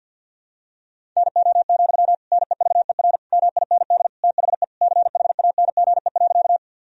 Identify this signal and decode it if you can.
Morse code, heard as a keyed tone, about 50 wpm, 700 Hz. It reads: NO, NEVER MIND THE CHANGE.
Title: Sentences from Top 1000 Words - Review Part 1 - 50wpm